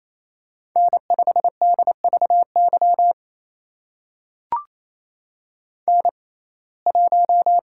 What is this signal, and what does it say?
Morse code, heard as a keyed tone, about 28 wpm, spaced as for 20 wpm, 700 Hz. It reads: N5DVY N1
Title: USA Call Signs - 5 Characters - 6x Character Spacing - 28wpm